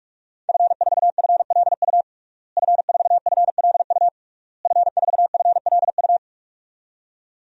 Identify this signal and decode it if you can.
Morse code, heard as a keyed tone, about 47 wpm, 700 Hz. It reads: F4FLU F4FLU F4FLU
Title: DXCC-100 Call Signs - 5 Characters - Repeated 3x - 45wpm